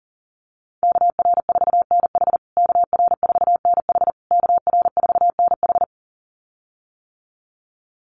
Morse code, heard as a keyed tone, about 37 wpm, 700 Hz. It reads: KR4NH KR4NH KR4NH
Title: USA Call Signs Repeated 3x 40wpm